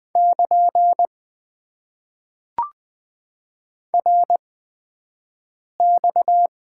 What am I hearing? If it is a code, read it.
Morse code, heard as a keyed tone, about 20 wpm, spaced as for 5 wpm, 700 Hz. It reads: <KN> RX
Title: Top 90 QSO Elements - 8x Character Spacing - 20wpm